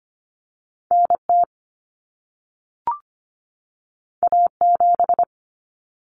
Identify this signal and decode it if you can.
Morse code, heard as a keyed tone, about 25 wpm, 700 Hz. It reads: NT A7
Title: Common Cut Numbers - 2-Digit 25wpm